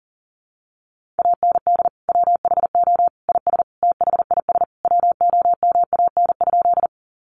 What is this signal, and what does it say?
Morse code, heard as a keyed tone, about 40 wpm, 700 Hz. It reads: AND WHO IS THIS WOMAN?
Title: Sentences from Top 1100 Words 40wpm